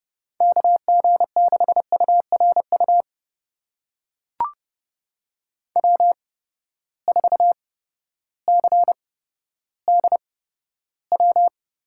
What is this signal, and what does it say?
Morse code, heard as a keyed tone, about 30 wpm, spaced as for 14 wpm, 700 Hz. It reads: KG6URU W4CDW
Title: USA Call Signs - 8x Character Spacing - 30wpm